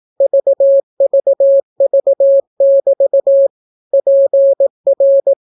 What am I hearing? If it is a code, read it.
Morse code, heard as a keyed tone, about 18 wpm, 550 Hz. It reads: VVV<BT> PR